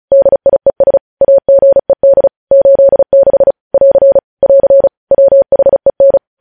Morse code, heard as a keyed tone, about 36 wpm, 550 Hz. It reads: DIES AGED 86 <AR> <AR> WHEN